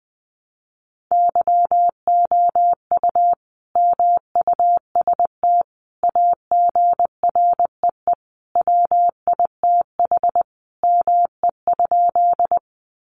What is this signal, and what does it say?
Morse code, heard as a keyed tone, about 20 wpm, 700 Hz. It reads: YOU MUST AGREE WITH ME?
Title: Sentences from Top 1100 Words 20wpm